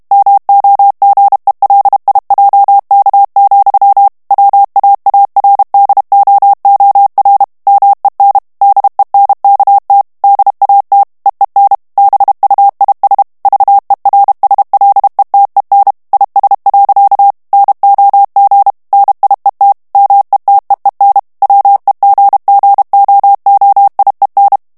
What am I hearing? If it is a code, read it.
Morse code, heard as a keyed tone, about 32 wpm, 800 Hz. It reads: MOGELIJK, WAARDOOR MEN DENKT DAT EEN BUIS VERSLETEN IS. NOG NIET METEEN WEGGOOIEN
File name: HOLLANDAIS texte clair 2 vitesse 30 mots minute